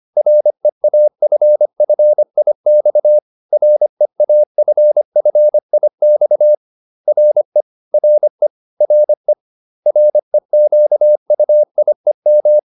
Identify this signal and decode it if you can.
Morse code, heard as a keyed tone, about 25 wpm, 600 Hz. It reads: REAFFIX REAFFIX RE RE RE REQUIEM